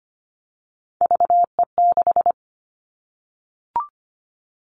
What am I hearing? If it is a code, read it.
Morse code, heard as a keyed tone, about 25 wpm, 700 Hz. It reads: VE6
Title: Callsign Prefixes for Top 50 DXCC - 25wpm